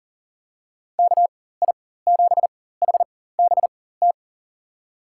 Morse code, heard as a keyed tone, about 40 wpm, spaced as for 19 wpm, 700 Hz. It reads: KI7HBT
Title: USA Call Signs - 6 Characters - 4x Character Spacing - 40wpm